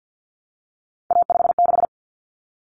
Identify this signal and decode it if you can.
Morse code, heard as a keyed tone, about 50 wpm, 700 Hz. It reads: A56
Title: Common Cut Numbers - 3-Digit 50wpm